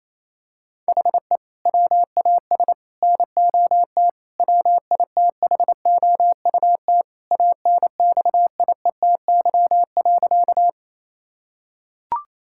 Morse code, heard as a keyed tone, about 28 wpm, 700 Hz. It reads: HE WAS NOT WITHOUT ANXIETY.
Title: Sentences from Top 1600 Words 28wpm